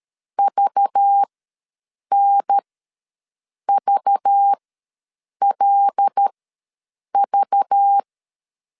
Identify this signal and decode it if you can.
Morse code, heard as a keyed tone, about 13 wpm, spaced as for 7 wpm, 800 Hz. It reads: VNVLV
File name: LEZIONI / 5 LBVC / 5 LBV ws 12wpm 3015